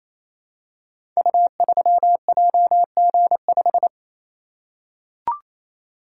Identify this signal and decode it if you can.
Morse code, heard as a keyed tone, about 27 wpm, 700 Hz. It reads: U3JG5
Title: Code Groups - 5 Letter:Numbers - 28wpm